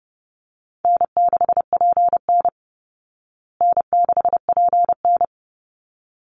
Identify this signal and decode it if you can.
Morse code, heard as a keyed tone, about 28 wpm, 700 Hz. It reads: N6PN N6PN
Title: Easy Call Signs Repeated 2x - 30wpm